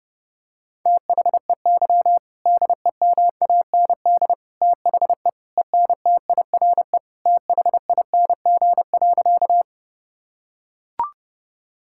Morse code, heard as a keyed tone, about 30 wpm, 700 Hz. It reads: THEY DEMAND THE ENTIRE THING.
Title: Sentences from Top 1000 Words - Review Part 5 - 30wpm